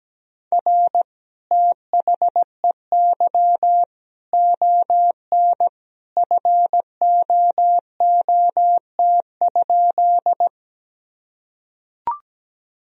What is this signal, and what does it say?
Morse code, heard as a keyed tone, about 17 wpm, 700 Hz. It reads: R THEY ON FOOT?
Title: Sentences from Top 500 Words and Abbreviations 17wpm